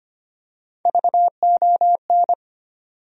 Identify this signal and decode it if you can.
Morse code, heard as a keyed tone, about 25 wpm, 700 Hz. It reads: VON
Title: USA Call Sign Postfixes - 3 Letters 25wpm